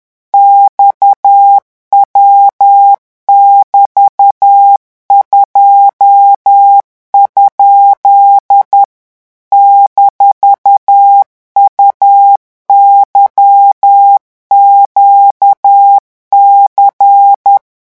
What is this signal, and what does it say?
Morse code, heard as a keyed tone, about 10 wpm, 800 Hz. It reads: XW<BT>2? -UYQC
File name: morse mp3 / revision17-vitesse 10 mots